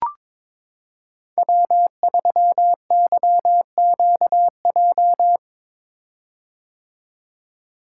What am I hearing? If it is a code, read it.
Morse code, heard as a keyed tone, about 22 wpm, 700 Hz. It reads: W3YQJ
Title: Code Groups - 5 Letter:Numbers - 22wpm